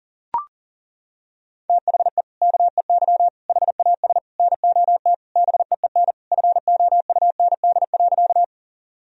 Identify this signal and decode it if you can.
Morse code, heard as a keyed tone, about 40 wpm, 700 Hz. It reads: THE KEY HAS NOT BEEN FOUND.
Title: Sentences from Top 1000 Words - Review Part 4 - 40wpm